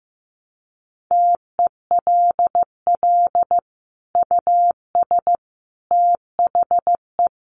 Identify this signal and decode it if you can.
Morse code, heard as a keyed tone, about 14 wpm, 700 Hz. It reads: TELL US THE
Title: Sentences from Top 1200 Words 15wpm